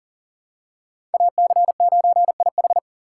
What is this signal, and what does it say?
Morse code, heard as a keyed tone, about 40 wpm, 700 Hz. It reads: AC9IH
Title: USA Call Signs - 5 Characters - 4x Character Spacing - 40wpm